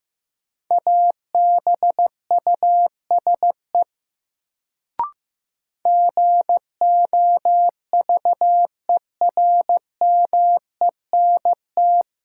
Morse code, heard as a keyed tone, about 15 wpm, 700 Hz. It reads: ABUSE GOVERMENT
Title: Sentences from Top 1300 Words - New Word Drill Alt - 15wpm